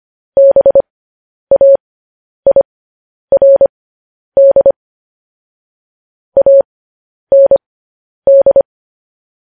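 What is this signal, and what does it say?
Morse code, heard as a keyed tone, about 25 wpm, spaced as for 10 wpm, 550 Hz. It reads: BAIRD AND
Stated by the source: Morse code news headlines at 10.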